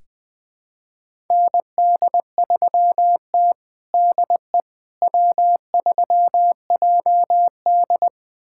Morse code, heard as a keyed tone, about 21 wpm, 700 Hz. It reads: ND3T DE W3JD
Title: USA Callsign DE Callsign - 4 Characters - 20wpm